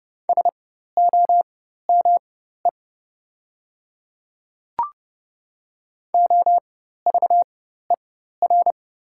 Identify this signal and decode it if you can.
Morse code, heard as a keyed tone, about 30 wpm, spaced as for 14 wpm, 700 Hz. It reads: SOME OVER
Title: 4x Character Spacing - Top 100 Words - 30wpm